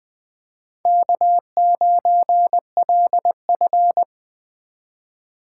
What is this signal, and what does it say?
Morse code, heard as a keyed tone, about 21 wpm, 700 Hz. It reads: K9LF
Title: Speed Racing - USA 1x2 Call Signs 17wpm